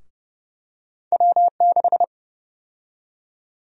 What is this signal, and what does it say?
Morse code, heard as a keyed tone, about 30 wpm, 700 Hz. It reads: W6
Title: USA Call Sign Prefixes - 30wpm